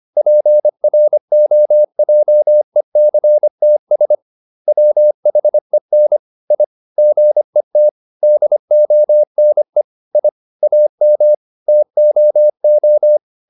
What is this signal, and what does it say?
Morse code, heard as a keyed tone, about 25 wpm, 600 Hz. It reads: PROJECTS WHEN I GET DONE I AM TOO